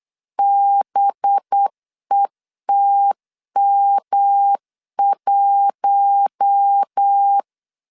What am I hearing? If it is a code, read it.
Morse code, heard as a keyed tone, about 8 wpm, 800 Hz. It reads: BETM1
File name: LEZIONI / 6 PKX0 / 6 PKX 8wpm 3041